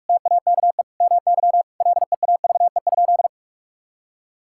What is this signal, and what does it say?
Morse code, heard as a keyed tone, about 46 wpm, 700 Hz. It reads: TAKE MY LEAVE?
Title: Sentences from Top 1000 Words - Review Part 5 - 45wpm